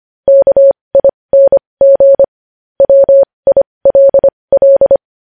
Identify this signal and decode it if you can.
Morse code, heard as a keyed tone, about 25 wpm, spaced as for 20 wpm, 550 Hz. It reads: KING WILL
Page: News Headlines in Morse Code at 20 WPM